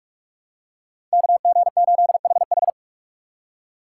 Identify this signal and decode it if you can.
Morse code, heard as a keyed tone, about 44 wpm, 690 Hz. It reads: KG8HH
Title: USA Call Signs - 5 Characters - 2x Character Spacing - 45wpm